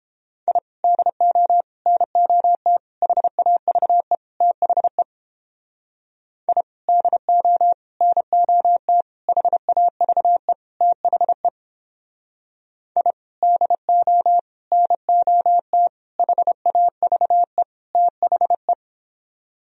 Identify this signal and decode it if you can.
Morse code, heard as a keyed tone, about 29 wpm, 700 Hz. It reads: I DO NOT HAVE THE I DO NOT HAVE THE I DO NOT HAVE THE